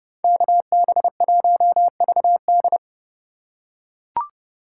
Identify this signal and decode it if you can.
Morse code, heard as a keyed tone, about 28 wpm, 700 Hz. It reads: KB1VD